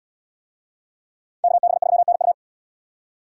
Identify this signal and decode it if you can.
Morse code, heard as a keyed tone, about 77 wpm, 700 Hz. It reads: KB3NU